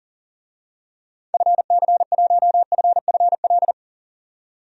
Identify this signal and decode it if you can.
Morse code, heard as a keyed tone, about 40 wpm, 700 Hz. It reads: FC1FFL
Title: DXCC-100 Call Signs - 6 Characters - 40wpm